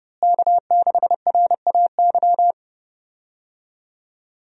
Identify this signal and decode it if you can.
Morse code, heard as a keyed tone, about 30 wpm, 700 Hz. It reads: K6RAY
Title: USA Call Signs - 5 Characters - Repeated 2x - 30wpm